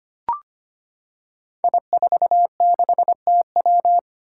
Morse code, heard as a keyed tone, about 25 wpm, 700 Hz. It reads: I46TW
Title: Five Challenging Letter-Number Combinations 25wpm